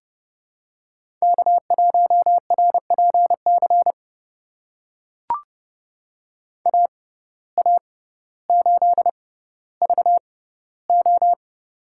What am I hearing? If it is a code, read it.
Morse code, heard as a keyed tone, about 30 wpm, spaced as for 15 wpm, 700 Hz. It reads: K1RPC AA8VO
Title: USA Call Signs - 5 Characters - 6x Character Spacing - 30wpm